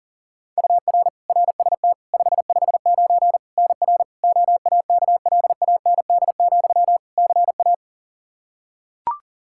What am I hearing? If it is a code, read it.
Morse code, heard as a keyed tone, about 40 wpm, 700 Hz. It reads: UR RST 559 NR OAKLAND, CA
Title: K1USN SST QSO Practice 40wpm